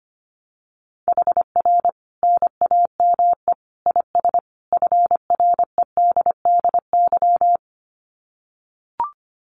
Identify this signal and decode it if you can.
Morse code, heard as a keyed tone, about 25 wpm, 700 Hz. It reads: HR NAME IS FREDDY
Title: Common QSO Phrases 25wpm